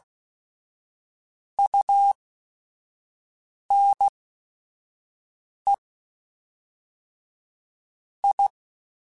Morse code, heard as a keyed tone, about 16 wpm, spaced as for 5 wpm, 800 Hz. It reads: UNE I